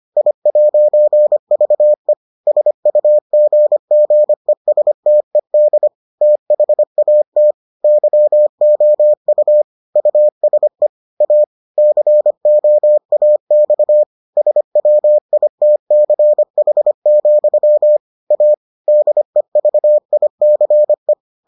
Morse code, heard as a keyed tone, about 25 wpm, 600 Hz. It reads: I'VE SUGGESTED THAT YOU USE A COAX SWITCH, A DEVICE